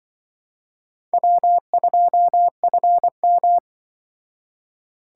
Morse code, heard as a keyed tone, about 24 wpm, 700 Hz. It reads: W2FM
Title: Speed Racing - USA Call Signs 20wpm